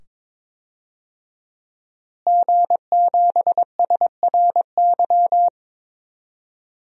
Morse code, heard as a keyed tone, about 22 wpm, 700 Hz. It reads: G7SRY
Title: DXCC-100 Call Signs - 5 Characters - Repeated 2x - 22wpm